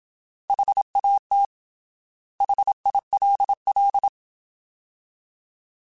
Morse code, heard as a keyed tone, about 27 wpm, 800 Hz. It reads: HAT HILL
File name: Alphabetical / Alphabetical mp3s / Hs @25